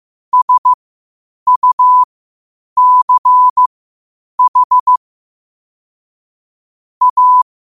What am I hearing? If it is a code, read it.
Morse code, heard as a keyed tone, about 15 wpm, spaced as for 8 wpm, 1000 Hz. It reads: SUCH A
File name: qotdfiles / 8wpm